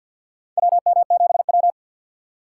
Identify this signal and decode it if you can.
Morse code, heard as a keyed tone, about 53 wpm, 690 Hz. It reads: WM7W
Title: USA Call Signs - 4 Characters - 6x Character Spacing - 50wpm